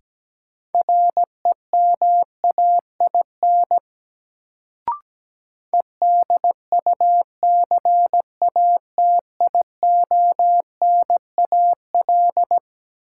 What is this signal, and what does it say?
Morse code, heard as a keyed tone, about 17 wpm, 700 Hz. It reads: REMAIN EDUCATIONAL
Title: Sentences from Top 1200 Words - New Word Drill - 17wpm